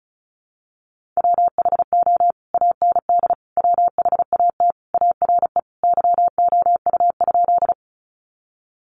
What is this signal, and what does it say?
Morse code, heard as a keyed tone, about 34 wpm, 700 Hz. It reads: WHO AND WHAT ARE YOU?
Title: Sentences from Top 100 Words 35wpm